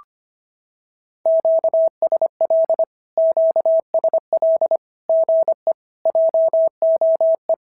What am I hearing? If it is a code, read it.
Morse code, heard as a keyed tone, about 25 wpm, 650 Hz. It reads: QSL QSL GE JOE
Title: SOTA QSO Practice 25wpm